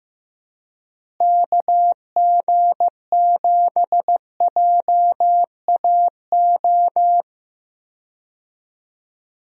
Morse code, heard as a keyed tone, about 15 wpm, 700 Hz. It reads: KG7JAO
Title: USA Call Signs - 6 Characters - Repeated 3x - 15wpm